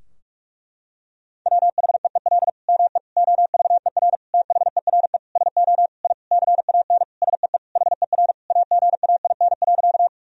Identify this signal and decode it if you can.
Morse code, heard as a keyed tone, about 46 wpm, 700 Hz. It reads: WHEEL ME OVER THERE SO I CAN SEE HER AGAIN.
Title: Sentences from Top 1000 Words - Review Part 3 - 45wpm